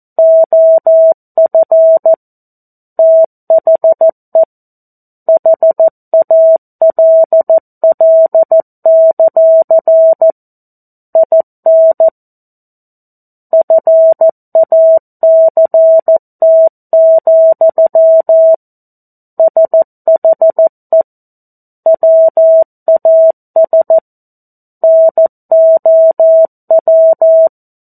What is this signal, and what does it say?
Morse code, heard as a keyed tone, about 14 wpm, 650 Hz. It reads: OF THE HALL; IN FACT, SHE WAS NOW
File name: 02-Chapter-02 14 WPM